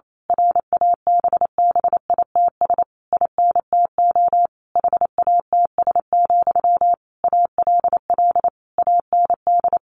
Morse code, heard as a keyed tone, about 28 wpm, 700 Hz. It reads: RABBITS INTO HATS, ALL AND
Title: A Sound Of Thunder 28wpm